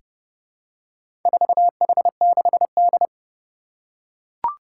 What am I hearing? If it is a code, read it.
Morse code, heard as a keyed tone, about 30 wpm, 700 Hz. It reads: 4H6D
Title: Four Challenging Letter-Number Combinations 30wpm